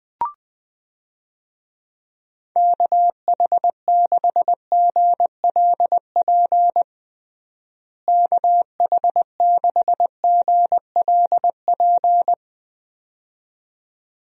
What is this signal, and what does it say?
Morse code, heard as a keyed tone, about 20 wpm, 700 Hz. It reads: KH6GLP KH6GLP
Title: DXCC-100 Call Signs - 6 Characters - Repeated 2x - 20wpm